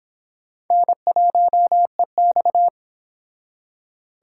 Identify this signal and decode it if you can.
Morse code, heard as a keyed tone, about 26 wpm, 700 Hz. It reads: N1EX